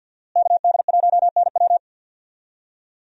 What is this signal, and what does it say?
Morse code, heard as a keyed tone, about 49 wpm, 690 Hz. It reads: KD1NW
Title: USA Call Signs - 5 Characters - Repeated 3x - 50wpm